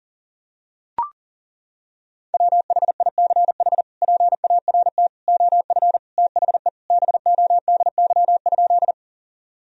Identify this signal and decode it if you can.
Morse code, heard as a keyed tone, about 40 wpm, 700 Hz. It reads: WHICH PART OF THE BODY?
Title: Sentences from Top 1000 Words - Review Part 2 - 40wpm